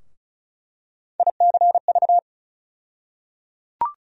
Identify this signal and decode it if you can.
Morse code, heard as a keyed tone, about 36 wpm, 700 Hz. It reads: ICV
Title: USA Call Sign Postfixes - 3 Letters 35wpm